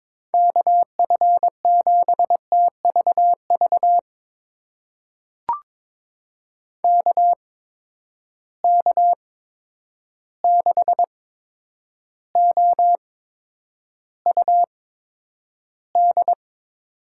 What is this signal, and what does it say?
Morse code, heard as a keyed tone, about 22 wpm, spaced as for 9 wpm, 700 Hz. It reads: KF7TVV KK6OUD
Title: USA Call Signs - 6 Characters - 8x Character Spacing - 22wpm